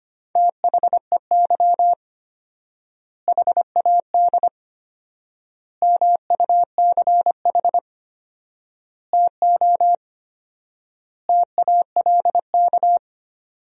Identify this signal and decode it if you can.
Morse code, heard as a keyed tone, about 25 wpm, 700 Hz. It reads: THEY HAD MUCH TO TALK
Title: 4x Word-Spacing - Sentences from Top 400 Words - 25wpm